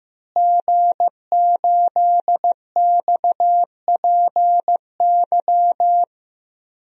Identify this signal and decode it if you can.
Morse code, heard as a keyed tone, about 15 wpm, 700 Hz. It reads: G8XPY
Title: DXCC-100 Call Signs - 5 Characters - Repeated 3x - 15wpm